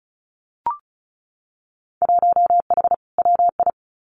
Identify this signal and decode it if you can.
Morse code, heard as a keyed tone, about 36 wpm, 700 Hz. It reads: 1H WI
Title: Winter Field Day Exchanges 35wpm